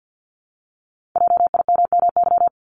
Morse code, heard as a keyed tone, about 50 wpm, 700 Hz. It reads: JIMMY
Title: Multi-letter Phonemes Repeated 3x 50wpm